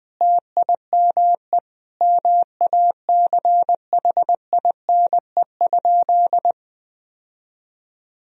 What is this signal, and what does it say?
Morse code, heard as a keyed tone, about 20 wpm, 700 Hz. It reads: TIME MACHINE?
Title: Sentences from Top 1000 Words - Review Part 2 - 20wpm